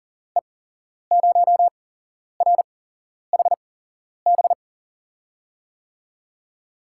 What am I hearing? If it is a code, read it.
Morse code, heard as a keyed tone, about 40 wpm, spaced as for 11 wpm, 700 Hz. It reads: E0RHB
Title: USA Call Signs - 6 Characters - 8x Character Spacing - 40wpm